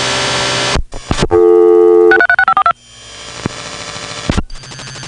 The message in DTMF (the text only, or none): 636542